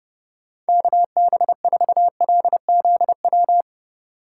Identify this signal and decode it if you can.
Morse code, heard as a keyed tone, about 30 wpm, 700 Hz. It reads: KB4LZW
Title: USA Call Signs - 6 Characters - 3x Character Spacing - 30wpm